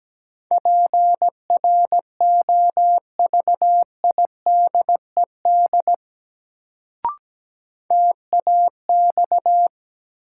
Morse code, heard as a keyed tone, about 17 wpm, 700 Hz. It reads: PROVIDED TAX
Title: Sentences from Top 1500 Words - New Word Drill Alt - 17wpm